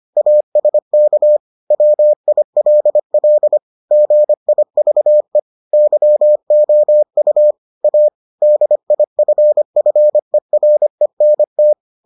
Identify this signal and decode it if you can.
Morse code, heard as a keyed tone, about 25 wpm, 600 Hz. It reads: ASK WILL GIVE YOU A DIFFERENT